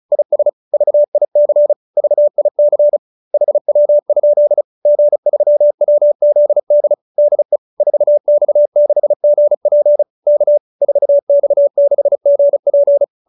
Morse code, heard as a keyed tone, about 34 wpm, 590 Hz. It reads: IS VIC VIC HW? G3WZD DE 4X6GP K 4X6GP
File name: QSO310 35